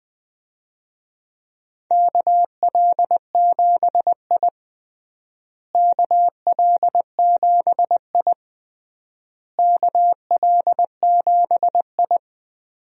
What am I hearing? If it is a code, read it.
Morse code, heard as a keyed tone, about 20 wpm, 700 Hz. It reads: KL7I KL7I KL7I
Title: DXCC-100 Call Signs - 4 Characters - Repeated 3x - 20wpm